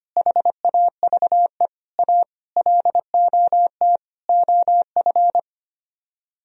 Morse code, heard as a keyed tone, about 29 wpm, 700 Hz. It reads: HAVE A LOT OF
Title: Speed-Racing - Sets of 5 Words 25wpm